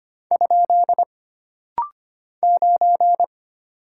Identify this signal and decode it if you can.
Morse code, heard as a keyed tone, about 25 wpm, 700 Hz. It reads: ? 9
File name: Single Character (Letter:Number:Punctuation:Prosign) - Rapid-Fire - 25wpm